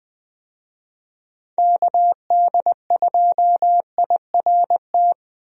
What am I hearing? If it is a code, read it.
Morse code, heard as a keyed tone, about 20 wpm, 700 Hz. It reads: KD2IRT
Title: USA Call Signs - 6 Characters - 20wpm